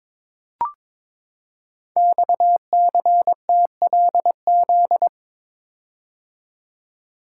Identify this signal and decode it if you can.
Morse code, heard as a keyed tone, about 23 wpm, 700 Hz. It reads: XCTLZ
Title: Code Groups - 5 Letters - 22wpm